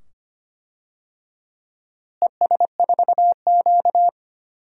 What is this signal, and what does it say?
Morse code, heard as a keyed tone, about 25 wpm, 700 Hz. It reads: ES4Q
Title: DXCC-100 Call Signs - 4 Characters - Repeated 2x - 25wpm